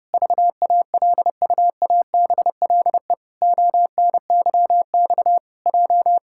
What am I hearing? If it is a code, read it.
Morse code, heard as a keyed tone, about 30 wpm, 700 Hz. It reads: VALUABLE ONYX J
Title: Pangram Challenge v2 - 30wpm